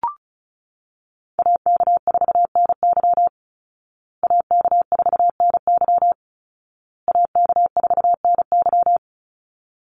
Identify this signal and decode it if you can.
Morse code, heard as a keyed tone, about 34 wpm, 700 Hz. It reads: AK4NY AK4NY AK4NY
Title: Call Signs Repeated 2x - 35wpm